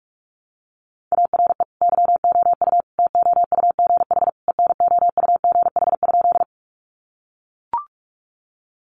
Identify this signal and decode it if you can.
Morse code, heard as a keyed tone, about 44 wpm, 700 Hz. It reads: ARE YOU TOUGH ENOUGH?
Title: Sentences from Top 1200 Words 45wpm